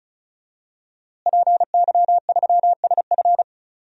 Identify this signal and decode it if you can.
Morse code, heard as a keyed tone, about 36 wpm, 700 Hz. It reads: PY3SF
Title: DXCC-100 Call Signs - 5 Characters - Repeated 2x - 35wpm